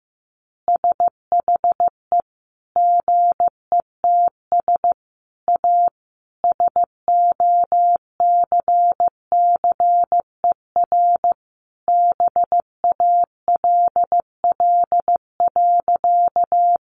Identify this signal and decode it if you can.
Morse code, heard as a keyed tone, about 15 wpm, 700 Hz. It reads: SHE GETS A SOCCER BALL.
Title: Easy Phrases 15wpm